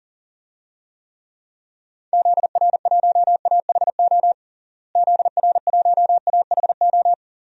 Morse code, heard as a keyed tone, about 40 wpm, 700 Hz. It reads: ZR1AHO ZR1AHO
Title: DXCC-100 Call Signs - 6 Characters - Repeated 2x - 40wpm